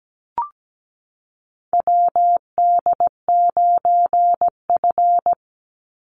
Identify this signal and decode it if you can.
Morse code, heard as a keyed tone, about 17 wpm, 700 Hz. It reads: WD9F
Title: NA SOTA Chaser Callsigns 17wpm